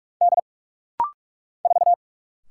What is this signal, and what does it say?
Morse code, heard as a keyed tone, about 44 wpm, 700 Hz. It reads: D 4
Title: Single Letter-Number - Rapid-Fire 45wpm